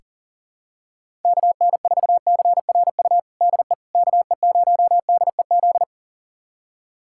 Morse code, heard as a keyed tone, about 40 wpm, 700 Hz. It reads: KN4CRU DE KE0DEZ
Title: USA Callsign DE Callsign - 6 Characters - 40wpm